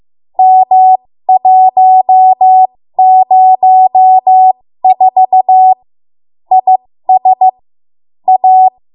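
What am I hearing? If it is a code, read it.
Morse code, heard as a keyed tone, about 15 wpm, spaced as for 13 wpm, 750 Hz. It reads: M104 IS A
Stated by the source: 13 wpm Morse code practice archive transmission.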